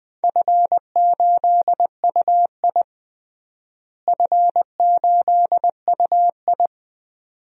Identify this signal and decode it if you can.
Morse code, heard as a keyed tone, about 21 wpm, 700 Hz. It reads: F8UI F8UI
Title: DXCC-100 Call Signs - 4 Characters - Repeated 2x - 20wpm